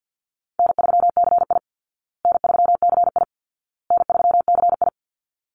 Morse code, heard as a keyed tone, about 48 wpm, 700 Hz. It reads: N3CI N3CI N3CI
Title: Easy Call Signs Repeated 3x - 50wpm